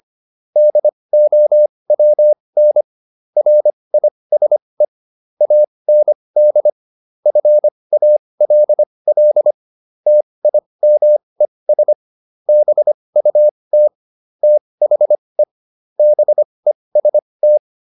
Morse code, heard as a keyed tone, about 25 wpm, spaced as for 20 wpm, 600 Hz. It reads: DOWN RISE AND FALL TIMES BUT THE BEST